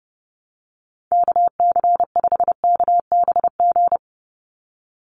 Morse code, heard as a keyed tone, about 30 wpm, 700 Hz. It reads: KC5KBG
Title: USA Call Signs 30wpm